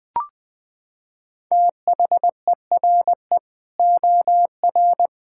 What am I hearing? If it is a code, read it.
Morse code, heard as a keyed tone, about 20 wpm, 700 Hz. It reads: THERE OR
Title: Sentences from Top 1600 Words 20wpm